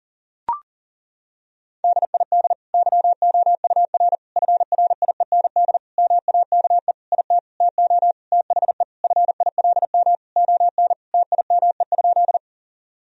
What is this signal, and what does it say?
Morse code, heard as a keyed tone, about 40 wpm, 700 Hz. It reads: DID YOUR FRIEND MAKE IT TO THE FILM ON TIME?
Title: Sentences from Top 1000 Words - Review Part 3 - 40wpm